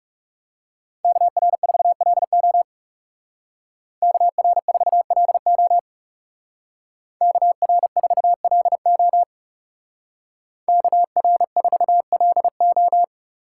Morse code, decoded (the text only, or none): KR4LO KR4LO KR4LO KR4LO